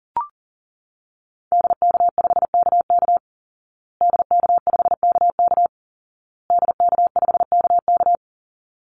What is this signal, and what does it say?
Morse code, heard as a keyed tone, about 40 wpm, 700 Hz. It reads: DK5KK DK5KK DK5KK
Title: Call Signs Repeated 2x - 40wpm